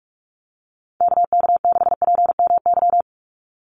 Morse code, heard as a keyed tone, about 44 wpm, 700 Hz. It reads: KK6PMY
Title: USA Call Signs 45wpm